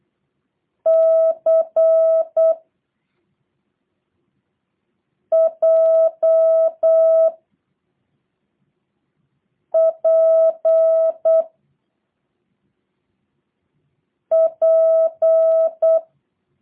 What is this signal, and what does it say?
Morse code, heard as a keyed tone, about 8 wpm, 650 Hz. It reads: C J P P
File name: jh8cbh / 2023cw / review1-5